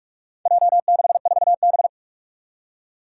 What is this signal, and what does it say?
Morse code, heard as a keyed tone, about 46 wpm, 700 Hz. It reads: J64B
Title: Four Challenging Letter-Number Combinations 45wpm